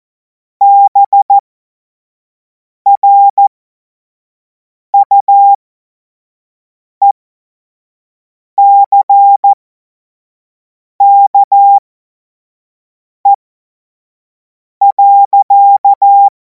Morse code, heard as a keyed tone, about 14 wpm, spaced as for 5 wpm, 800 Hz. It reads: BRUECKE.
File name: Morsekurs / Witze / Witz 19. Mai 2026 25 F70